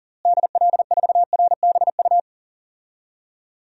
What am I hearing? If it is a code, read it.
Morse code, heard as a keyed tone, about 37 wpm, 700 Hz. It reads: DL4RBU
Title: DXCC-100 Call Signs - 6 Characters - Repeated 2x - 40wpm